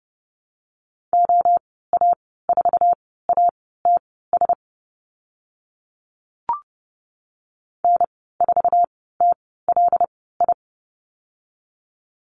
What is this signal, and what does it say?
Morse code, heard as a keyed tone, about 30 wpm, spaced as for 17 wpm, 690 Hz. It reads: OA4ATS N4TLI
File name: Lesson 15g - Number 5 - Call Signs 30wpm